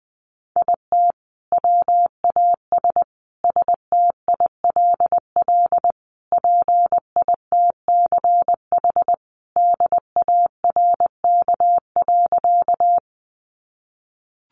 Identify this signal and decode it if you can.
Morse code, heard as a keyed tone, about 20 wpm, 700 Hz. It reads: IT WAS STILL PITCH DARK.
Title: Sentences from Top 1300 Words 20wpm